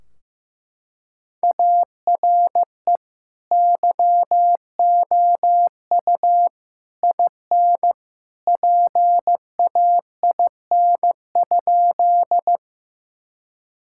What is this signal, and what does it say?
Morse code, heard as a keyed tone, about 15 wpm, 700 Hz. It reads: ARE YOU IN PAIN?
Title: Sentences from Top 1600 Words 15wpm